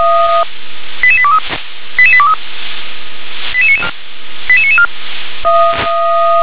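Five-tone selective call (ZVEI), heard as quick runs of five tones, tones of about 70 ms. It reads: 88823 88823 88994
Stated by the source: Signal received via SDR, ZVEI five-tone.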